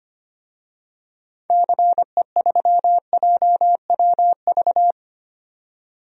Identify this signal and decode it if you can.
Morse code, heard as a keyed tone, about 25 wpm, 700 Hz. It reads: CE3JWV
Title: DXCC-100 Call Signs - 6 Characters - 25wpm